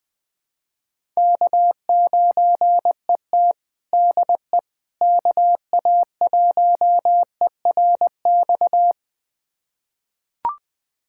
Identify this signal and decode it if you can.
Morse code, heard as a keyed tone, about 20 wpm, 700 Hz. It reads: K9ET DE KA1ERX
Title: USA Callsign DE Callsign 20wpm